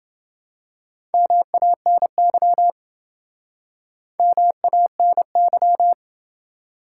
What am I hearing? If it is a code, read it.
Morse code, heard as a keyed tone, about 28 wpm, 700 Hz. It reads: MANY MANY